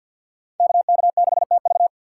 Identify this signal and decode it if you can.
Morse code, heard as a keyed tone, about 50 wpm, 700 Hz. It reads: KK6TV
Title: K1USN SST QSO Practice - 50wpm